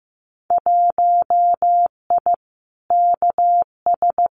Morse code, heard as a keyed tone, about 15 wpm, 700 Hz. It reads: 1I KS
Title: Winter Field Day Exchanges Encore 15wpm